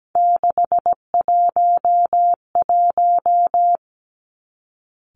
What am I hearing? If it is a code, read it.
Morse code, heard as a keyed tone, about 17 wpm, 700 Hz. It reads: 611
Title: Three Challenging Letter-Number Combinations 17wpm